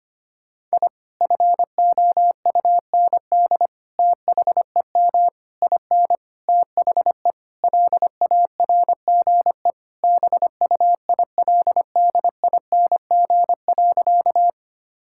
Morse code, heard as a keyed tone, about 25 wpm, 700 Hz. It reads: I FOUND THEM IN THE LARGE BUILDING.
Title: Sentences from Top 1000 Words - Review Part 5 - 25wpm